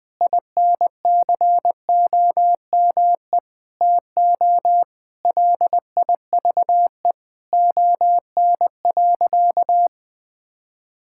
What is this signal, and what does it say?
Morse code, heard as a keyed tone, about 21 wpm, 700 Hz. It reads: INCOME TO LIVE ON.
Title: Sentences from Top 1000 Words - Review Part 5 - 20wpm